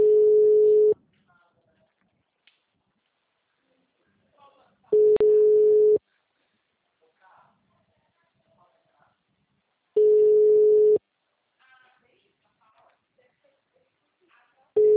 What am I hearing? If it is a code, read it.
Ringback tone (Europe, 425 Hz), with speech.
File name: out- -2007- - - .28686